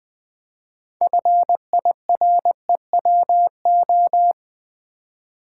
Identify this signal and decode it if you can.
Morse code, heard as a keyed tone, about 20 wpm, 700 Hz. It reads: FIREWO
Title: Build A Word - 3 Letters At A Time - 8-Letter Words - 20wpm